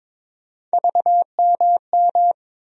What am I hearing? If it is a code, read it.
Morse code, heard as a keyed tone, about 22 wpm, 700 Hz. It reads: VMM
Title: USA Call Sign Postfixes - 3 Letters 22wpm